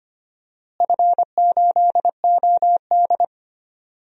Morse code, heard as a keyed tone, about 25 wpm, 700 Hz. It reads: F8OD
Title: DXCC-100 Call Signs - 4 Characters - Repeated 2x - 25wpm